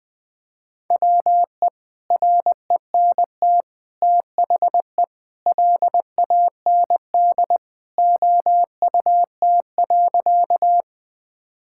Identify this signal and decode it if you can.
Morse code, heard as a keyed tone, about 20 wpm, 700 Hz. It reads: WE RENT THE LAND OUT.
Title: Sentences from Top 1000 Words - Review Part 5 - 20wpm